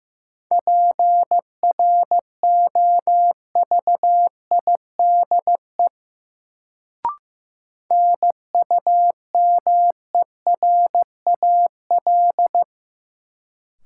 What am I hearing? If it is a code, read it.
Morse code, heard as a keyed tone, about 16 wpm, 700 Hz. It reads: PROVIDE NUMERAL
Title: Sentences from Top 1000 Words - Review Part 2 - New Word Drill - 15wpm